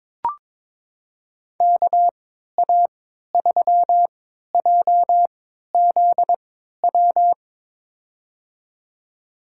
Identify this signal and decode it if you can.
Morse code, heard as a keyed tone, about 22 wpm, spaced as for 13 wpm, 700 Hz. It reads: KA3JZW
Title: USA Call Signs - 6 Characters - 3x Character Spacing - 22wpm